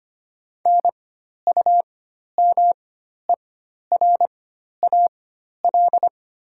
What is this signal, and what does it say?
Morse code, heard as a keyed tone, about 25 wpm, spaced as for 12 wpm, 700 Hz. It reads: NUMERAL
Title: 4x Character Spacing - Top 500 Words - 25wpm